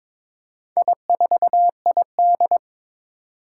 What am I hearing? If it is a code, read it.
Morse code, heard as a keyed tone, about 22 wpm, 700 Hz. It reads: I4ID
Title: Four Challenging Letter-Number Combinations 22wpm